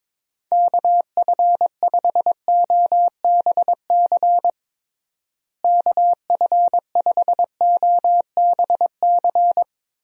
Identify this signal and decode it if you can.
Morse code, heard as a keyed tone, about 22 wpm, 700 Hz. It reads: KF5OBC KF5OBC
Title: USA Call Signs - 6 Characters - Repeated 2x - 22wpm